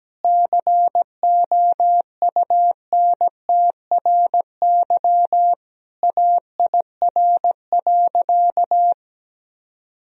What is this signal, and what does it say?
Morse code, heard as a keyed tone, about 17 wpm, 700 Hz. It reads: COUNTRY AIR.
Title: Sentences from Top 1000 Words - Review Part 1 - 17wpm